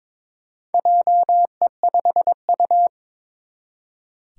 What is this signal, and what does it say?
Morse code, heard as a keyed tone, about 23 wpm, 700 Hz. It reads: JE5U
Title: Four Challenging Letter-Number Combinations 22wpm